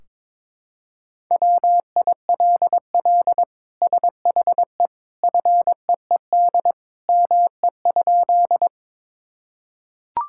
Morse code, heard as a keyed tone, about 22 wpm, 700 Hz. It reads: WILL SHE FEED ME?
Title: Sentences from Top 1600 Words 22wpm